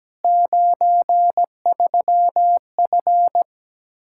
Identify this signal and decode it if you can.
Morse code, heard as a keyed tone, about 17 wpm, 700 Hz. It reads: 93F
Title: K1USN SST QSO Practice 17wpm